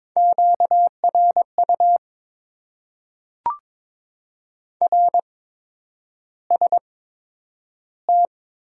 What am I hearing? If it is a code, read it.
Morse code, heard as a keyed tone, about 22 wpm, spaced as for 9 wpm, 700 Hz. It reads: QRU RST
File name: Top 90 QSO Elements - 8x Character Spacing - 22wpm